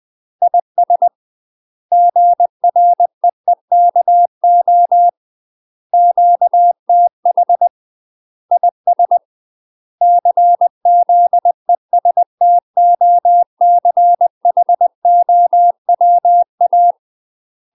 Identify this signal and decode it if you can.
Morse code, heard as a keyed tone, about 20 wpm, 700 Hz. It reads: IS GREEKO QTH IS CZESTOCHOWA
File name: UK MASTER QSOs TRAINING / 20WPM / UK MASTER QSOs 0069